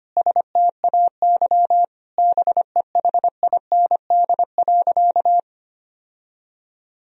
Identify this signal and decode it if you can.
Morse code, heard as a keyed tone, about 25 wpm, 700 Hz. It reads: STAY BEHIND.
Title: Sentences from Top 1000 Words - Review Part 1 - 25wpm